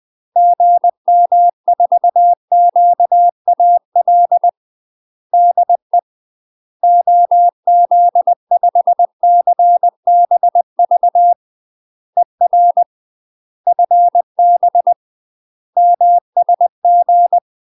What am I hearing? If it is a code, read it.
Morse code, heard as a keyed tone, about 20 wpm, 700 Hz. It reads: GM4QAL DE OZ5CBV ER FB MSG
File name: UK MASTER QSOs TRAINING/20WPM